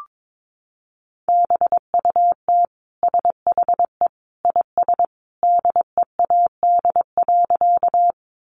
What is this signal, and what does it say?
Morse code, heard as a keyed tone, about 22 wpm, 700 Hz. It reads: BUT SHE IS DEAD.
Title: Sentences from Top 1500 Words 22wpm